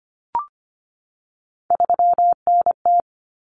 Morse code, heard as a keyed tone, about 25 wpm, 700 Hz. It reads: 3NT
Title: Common Cut Numbers - 3-Digit 25wpm